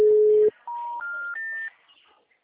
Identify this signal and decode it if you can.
Special information tone.
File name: out- -2002- - - .1932